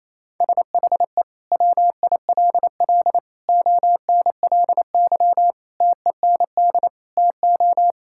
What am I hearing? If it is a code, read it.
Morse code, heard as a keyed tone, about 28 wpm, 700 Hz. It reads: SHE WILL ONLY TEND TO
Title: Sentences from Top 1000 Words - Review Part 5 - 28wpm